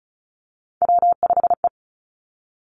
Morse code, heard as a keyed tone, about 36 wpm, 700 Hz. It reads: W5E
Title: Three Challenging Letter-Number Combinations 35wpm